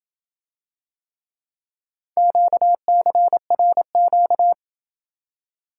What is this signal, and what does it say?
Morse code, heard as a keyed tone, about 27 wpm, 700 Hz. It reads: QCRQ